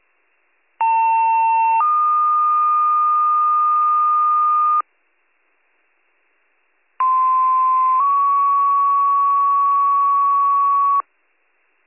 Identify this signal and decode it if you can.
Two-tone page, heard as a one-second tone, then a three-second tone, about 900 then 1220 Hz, about 1040 then 1120 Hz. It is heard from 0.8 to 11.0 s.